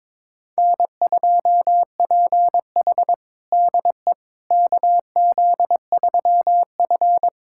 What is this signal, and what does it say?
Morse code, heard as a keyed tone, about 22 wpm, 700 Hz. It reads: N2PH DE KZ3F
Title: USA Callsign DE Callsign - 4 Characters - 22wpm